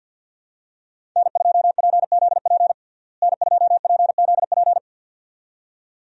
Morse code, heard as a keyed tone, about 52 wpm, 700 Hz. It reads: N2PZP N2PZP
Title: USA Call Signs - 5 Characters - Repeated 2x - 50wpm